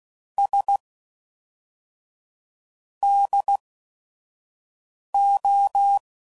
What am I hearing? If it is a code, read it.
Morse code, heard as a keyed tone, about 16 wpm, spaced as for 5 wpm, 800 Hz. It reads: S DO
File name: CW Code Groups @5WPM / QSOs / QSO WD0FBT